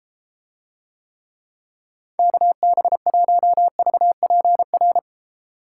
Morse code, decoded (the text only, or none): KB1VPR